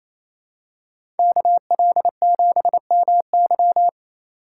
Morse code, heard as a keyed tone, about 28 wpm, 700 Hz. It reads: KL7MY
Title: USA Call Signs - 5 Characters - 2x Character Spacing - 28wpm